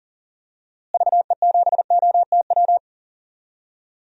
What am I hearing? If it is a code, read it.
Morse code, heard as a keyed tone, about 40 wpm, 700 Hz. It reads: VE7OTW
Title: DXCC-100 Call Signs - 6 Characters - Repeated 3x - 40wpm